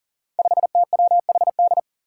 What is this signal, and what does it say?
Morse code, heard as a keyed tone, about 40 wpm, 700 Hz. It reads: 5TWHD
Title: Five Challenging Letter-Number Combinations 40wpm